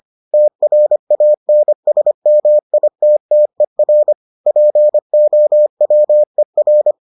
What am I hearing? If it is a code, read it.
Morse code, heard as a keyed tone, about 25 wpm, 600 Hz. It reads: TRANSMITTER POWER